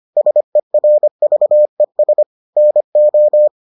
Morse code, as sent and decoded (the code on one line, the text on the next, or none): ... .-. ...- . ... / -. ---
SERVES NO